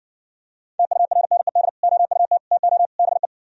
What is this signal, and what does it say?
Morse code, heard as a keyed tone, about 68 wpm, 700 Hz. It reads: TURNED OUT TO BE